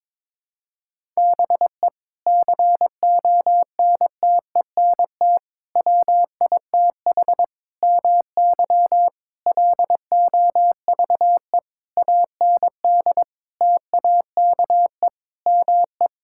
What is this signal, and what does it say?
Morse code, heard as a keyed tone, about 22 wpm, 700 Hz. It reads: BE CONTENT WITH MY LOVE AND TAKE ME
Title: Sentences from Top 1000 Words - Review Part 6 - 22wpm